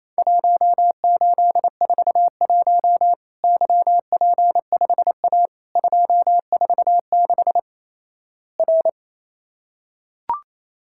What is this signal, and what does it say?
Morse code, heard as a keyed tone, about 28 wpm, 700 Hz. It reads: 1841 YP5A 246 R
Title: Worked All Europe DX Contest - QTC Exchange - 28wpm